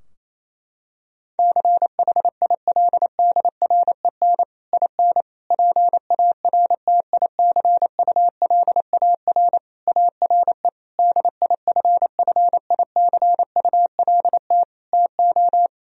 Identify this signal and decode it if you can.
Morse code, heard as a keyed tone, about 28 wpm, 700 Hz. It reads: CHILDREN IN PARTICULAR ARE DIFFICULT TO
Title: Sentences from Top 1000 Words - Review Part 4 - 28wpm